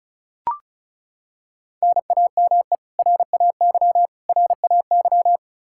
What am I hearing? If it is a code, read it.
Morse code, heard as a keyed tone, about 34 wpm, 700 Hz. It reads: NAME RAY RAY
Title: Practice QSO - Exchange 1 - Beginning - 35wpm